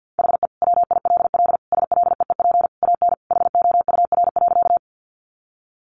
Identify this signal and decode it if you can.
Morse code, heard as a keyed tone, about 50 wpm, 700 Hz. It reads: HE WILL SLEEP AN HOUR.
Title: Sentences from Top 500 Words 50wpm